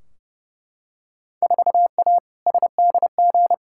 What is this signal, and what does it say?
Morse code, heard as a keyed tone, about 32 wpm, 700 Hz. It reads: 4A SDG
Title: Field Day Exchanges - 30wpm